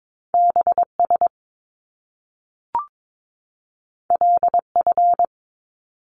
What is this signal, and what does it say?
Morse code, heard as a keyed tone, about 22 wpm, 700 Hz. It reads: BS LF
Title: Top Two Letter Combinations in Words 22wpm